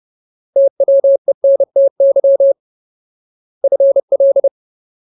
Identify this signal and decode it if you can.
Morse code, heard as a keyed tone, about 30 wpm, 550 Hz. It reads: TWENTY FL
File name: Ultimate Word List / 30WPM Ultimate